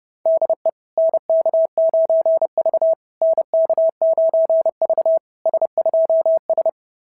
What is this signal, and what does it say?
Morse code, heard as a keyed tone, about 31 wpm, 690 Hz. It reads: DE NK9V NK9V S2S
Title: SOTA QSO Practice 30wpm